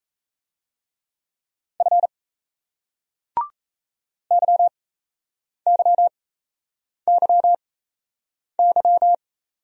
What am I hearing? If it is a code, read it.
Morse code, heard as a keyed tone, about 35 wpm, spaced as for 8 wpm, 700 Hz. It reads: F YYYY
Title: Speed-Racing Single Letters 28wpm